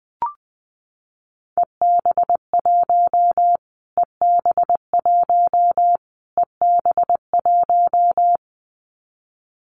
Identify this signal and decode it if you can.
Morse code, heard as a keyed tone, about 20 wpm, 700 Hz. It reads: EB1 EB1 EB1
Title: Callsign Prefixes for Top 25 DXCC - Repeated 3x - 20wpm